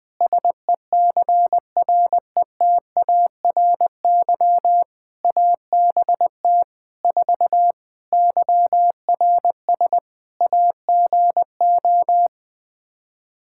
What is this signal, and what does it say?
Morse code, heard as a keyed tone, about 20 wpm, 700 Hz. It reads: SECRETARY ABT 4 YRS AGO
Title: Practice QSO - Exchange 3 - More Info - 20wpm